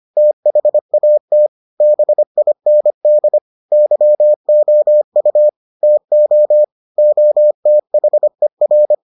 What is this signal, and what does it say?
Morse code, heard as a keyed tone, about 25 wpm, 600 Hz. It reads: THAT BIND YOU TO OTHER